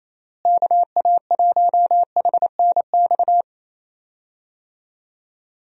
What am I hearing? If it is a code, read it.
Morse code, heard as a keyed tone, about 28 wpm, 700 Hz. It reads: KA1HNX
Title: USA Call Signs - 6 Characters - Repeated 2x - 28wpm